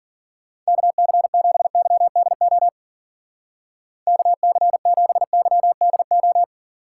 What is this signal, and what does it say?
Morse code, heard as a keyed tone, about 52 wpm, 700 Hz. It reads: KC7YDO KC7YDO